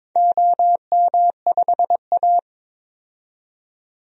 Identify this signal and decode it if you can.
Morse code, heard as a keyed tone, about 23 wpm, 700 Hz. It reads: OM5A
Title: DXCC-100 Call Signs - 4 Characters - 22wpm